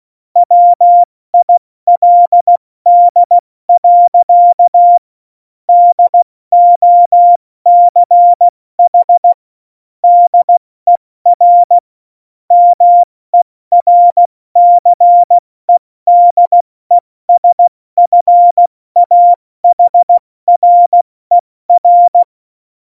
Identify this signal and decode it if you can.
Morse code, heard as a keyed tone, about 16 wpm, 700 Hz. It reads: WILD. DOCH DER MERCEDESFAHRER